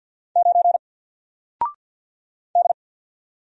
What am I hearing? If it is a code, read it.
Morse code, heard as a keyed tone, about 50 wpm, 700 Hz. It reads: 9 D
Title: Single Character (Letter:Number:Punctuation:Prosign) 50wpm